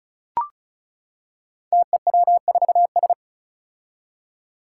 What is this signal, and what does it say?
Morse code, heard as a keyed tone, about 35 wpm, 700 Hz. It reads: TEW4S
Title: Five Challenging Letter-Number Combinations 35wpm